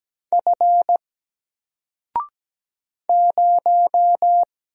Morse code, heard as a keyed tone, about 17 wpm, 700 Hz. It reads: F 0
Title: Single Character (Letter:Number:Punctuation:Prosign) Alt 17wpm